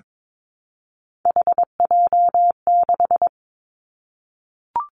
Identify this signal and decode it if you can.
Morse code, heard as a keyed tone, about 22 wpm, 700 Hz. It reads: HJ6
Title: Callsign Prefixes for Top 100 DXCC - 22wpm